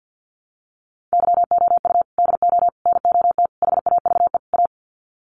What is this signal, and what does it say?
Morse code, heard as a keyed tone, about 50 wpm, 700 Hz. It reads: YOU DO NOT HAVE A
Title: Sets of 5 Words 50wpm